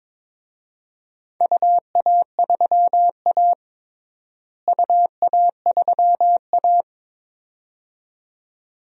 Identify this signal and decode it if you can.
Morse code, heard as a keyed tone, about 21 wpm, 700 Hz. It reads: UA3A UA3A
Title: DXCC-100 Call Signs - 4 Characters - Repeated 2x - 22wpm